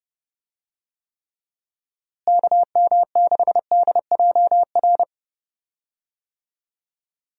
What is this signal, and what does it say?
Morse code, heard as a keyed tone, about 32 wpm, 700 Hz. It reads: KM6DJR
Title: USA Call Signs - 6 Characters - Repeated 3x - 30wpm